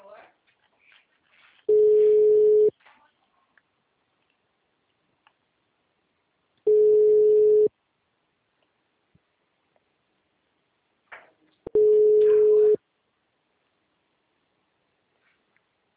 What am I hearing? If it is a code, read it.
Ringback tone (Europe, 425 Hz), with speech.